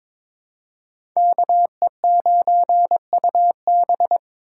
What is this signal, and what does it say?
Morse code, heard as a keyed tone, about 22 wpm, 700 Hz. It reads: KE9UB
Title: USA Call Signs - 5 Characters - 3x Character Spacing - 22wpm